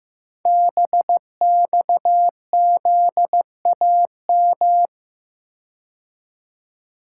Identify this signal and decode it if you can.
Morse code, heard as a keyed tone, about 15 wpm, 700 Hz. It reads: BXZAM
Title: Code Groups - 5 Letters - 15wpm